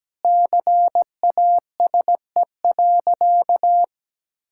Morse code, heard as a keyed tone, about 17 wpm, 700 Hz. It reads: CASE.
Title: Sentences from Top 1000 Words - Review Part 5 - 17wpm